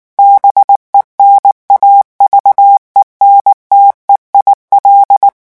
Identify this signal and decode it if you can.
Morse code, heard as a keyed tone, about 19 wpm, 800 Hz. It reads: BENAVENTE IL
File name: morse mp3 / revision9 citations humoristiques vitesse 18 mots minute